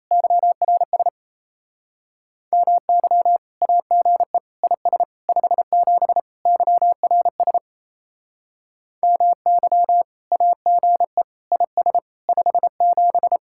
Morse code, heard as a keyed tone, about 34 wpm, 700 Hz. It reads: YRS MY AGE IS 57 YRS MY AGE IS 57
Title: Speed-Racing Common QSO Phrases - 28wpm